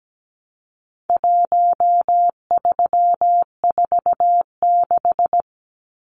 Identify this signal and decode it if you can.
Morse code, heard as a keyed tone, about 17 wpm, 700 Hz. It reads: 1346
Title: Numbers - Contest Style - 4-Digit 17wpm